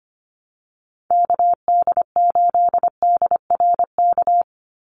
Morse code, heard as a keyed tone, about 25 wpm, 700 Hz. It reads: KD8DRK
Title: USA Call Signs 25wpm